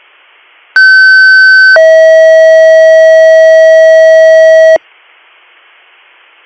Two-tone page, heard as a one-second tone, then a three-second tone, about 1550 then 640 Hz. It is heard from 0.8 to 4.8 s.